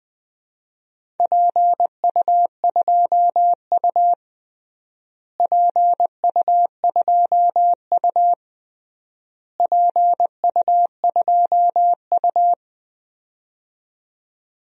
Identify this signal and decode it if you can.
Morse code, heard as a keyed tone, about 21 wpm, 700 Hz. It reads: PU2U PU2U PU2U
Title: DXCC-100 Call Signs - 4 Characters - Repeated 3x - 20wpm